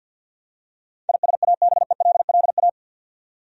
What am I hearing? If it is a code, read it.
Morse code, heard as a keyed tone, about 53 wpm, 700 Hz. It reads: ISABELLA
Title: USA Common Names - 50wpm